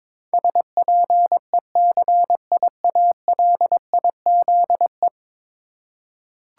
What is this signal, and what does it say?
Morse code, heard as a keyed tone, about 23 wpm, 700 Hz. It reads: SPECIALIZE
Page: morse-code-ninja-practice-22wpm